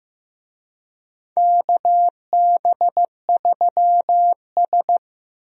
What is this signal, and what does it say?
Morse code, heard as a keyed tone, about 15 wpm, 700 Hz. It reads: KB3S
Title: USA Call Signs - 4 Characters - 15wpm